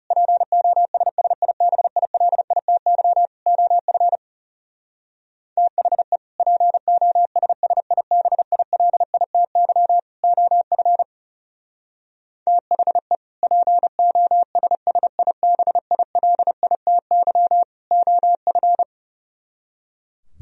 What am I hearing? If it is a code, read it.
Morse code, heard as a keyed tone, about 39 wpm, 700 Hz. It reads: POSSIBILITY OF THE POSSIBILITY OF THE POSSIBILITY OF